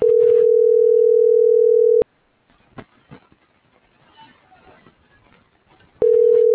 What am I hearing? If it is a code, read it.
Ringback tone (North America), 440 + 480 Hz.